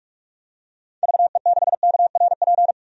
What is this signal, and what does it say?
Morse code, heard as a keyed tone, about 46 wpm, 700 Hz. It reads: VE6KRP
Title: DXCC-100 Call Signs - 6 Characters - 45wpm